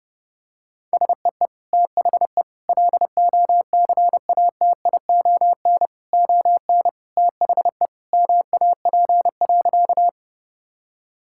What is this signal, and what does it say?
Morse code, heard as a keyed tone, about 30 wpm, 700 Hz. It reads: SEE THE LOCATION ON THE MAP.
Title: Sentences from Top 1000 Words - Review Part 6 - 30wpm